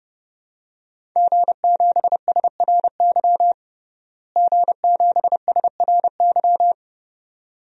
Morse code, heard as a keyed tone, about 30 wpm, 700 Hz. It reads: G7SRY G7SRY
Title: DXCC-100 Call Signs - 5 Characters - Repeated 2x - 30wpm